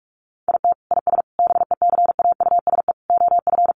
Morse code, heard as a keyed tone, about 44 wpm, 700 Hz. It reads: IT IS BECAUSE OF
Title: Sentences from Top 600 Words 45wpm